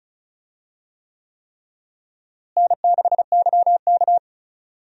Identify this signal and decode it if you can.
Morse code, heard as a keyed tone, about 36 wpm, 700 Hz. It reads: N6YK
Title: USA Call Signs - 4 Characters - Repeated 3x - 35wpm